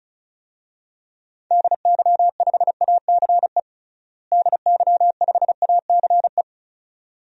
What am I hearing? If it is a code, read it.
Morse code, heard as a keyed tone, about 36 wpm, 700 Hz. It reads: DY5ACE DY5ACE
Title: DXCC-100 Call Signs - 6 Characters - Repeated 2x - 35wpm